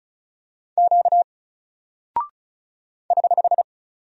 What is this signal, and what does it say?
Morse code, heard as a keyed tone, about 36 wpm, 700 Hz. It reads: Q <HH>
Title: Single Character (Letter:Number:Punctuation:Prosign) Alt 35wpm